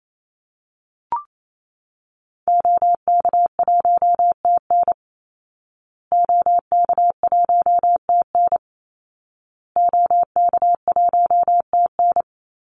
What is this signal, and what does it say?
Morse code, heard as a keyed tone, about 28 wpm, 700 Hz. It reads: OK1TN OK1TN OK1TN
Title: Call Signs Repeated 2x - 28wpm